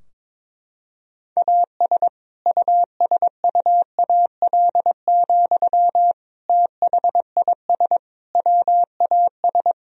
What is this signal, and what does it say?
Morse code, heard as a keyed tone, about 21 wpm, 700 Hz. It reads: AS USUAL, THIS WAS
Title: Sentences from Top 1000 Words - Review Part 2 - 22wpm